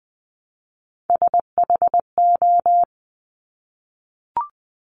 Morse code, heard as a keyed tone, about 20 wpm, 700 Hz. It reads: SHO
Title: Top Three Letter Combinations in Words 20wpm